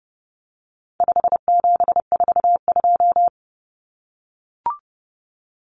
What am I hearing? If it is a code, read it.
Morse code, heard as a keyed tone, about 30 wpm, 700 Hz. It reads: 5742
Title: Common Cut Numbers - 4-Digit 30wpm